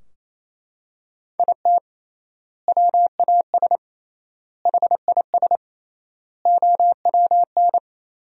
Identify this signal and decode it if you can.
Morse code, heard as a keyed tone, about 28 wpm, 700 Hz. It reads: IT WAS HIS OWN
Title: 3x Word-Spacing - Sentences from Top 300 Words - 28wpm